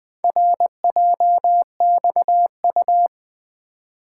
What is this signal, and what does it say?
Morse code, heard as a keyed tone, about 20 wpm, 700 Hz. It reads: RJXU
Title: Four Letter-Number Combinations - 20wpm